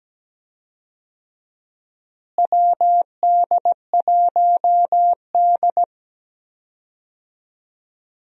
Morse code, heard as a keyed tone, about 17 wpm, 700 Hz. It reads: WD1D
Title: USA Call Signs - 4 Characters - Repeated 3x - 17wpm